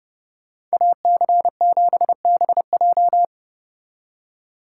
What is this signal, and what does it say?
Morse code, heard as a keyed tone, about 34 wpm, 700 Hz. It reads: AC7BJ